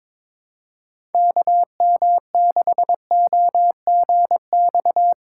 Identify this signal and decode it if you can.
Morse code, heard as a keyed tone, about 21 wpm, 700 Hz. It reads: KM6OGX
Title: USA Call Signs - 6 Characters - 3x Character Spacing - 22wpm